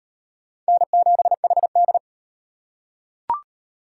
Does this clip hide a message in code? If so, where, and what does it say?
Yes: Morse code from 0.7 to 2.0 s, heard as a keyed tone, about 38 wpm, 700 Hz. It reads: N7HD